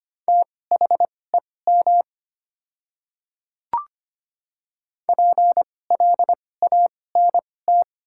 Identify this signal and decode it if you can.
Morse code, heard as a keyed tone, about 25 wpm, spaced as for 18 wpm, 700 Hz. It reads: THEM PLANT
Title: 2x Character Spacing - Top 300 Words - 25wpm